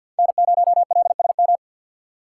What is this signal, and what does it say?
Morse code, heard as a keyed tone, about 46 wpm, 690 Hz. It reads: N0LSM
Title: USA Call Signs - 5 Characters - 3x Character Spacing - 50wpm